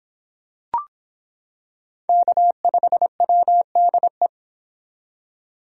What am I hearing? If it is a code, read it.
Morse code, heard as a keyed tone, about 25 wpm, 700 Hz. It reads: K5WDE